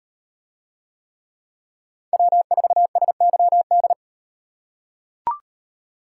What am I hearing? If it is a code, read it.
Morse code, heard as a keyed tone, about 38 wpm, 700 Hz. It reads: W4SYD